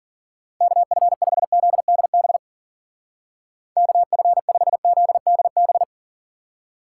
Morse code, heard as a keyed tone, about 52 wpm, 700 Hz. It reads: KF5ZDB KF5ZDB